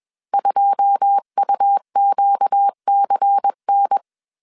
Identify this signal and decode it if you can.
Morse code, heard as a keyed tone, about 21 wpm, 800 Hz. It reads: 2UQCN
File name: CW / DET / 1 9 1500C S C 20wpm 2310